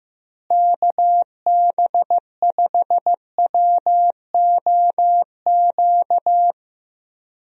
Morse code, heard as a keyed tone, about 15 wpm, 700 Hz. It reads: KB5WOQ
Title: USA Call Signs - 6 Characters - Repeated 3x - 15wpm